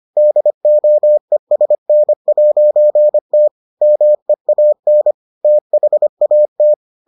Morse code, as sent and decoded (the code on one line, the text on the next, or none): -.. --- . ... -. .----. - / -- . .- -. / - .... .- -
DOESN'T MEAN THAT